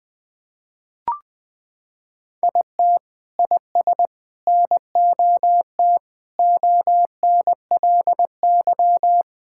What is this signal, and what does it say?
Morse code, heard as a keyed tone, about 21 wpm, 700 Hz. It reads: IT IS NOT ONLY
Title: Sentences from Top 1600 Words 20wpm